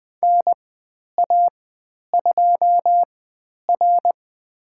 Morse code, heard as a keyed tone, about 20 wpm, spaced as for 10 wpm, 700 Hz. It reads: NA2R
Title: 102 Lesson 17g - Letter u - Call Signs 20wpm